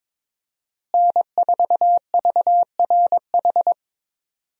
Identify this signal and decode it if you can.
Morse code, heard as a keyed tone, about 22 wpm, 700 Hz. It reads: N4VRH
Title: USA Call Signs - 5 Characters - Repeated 2x - 22wpm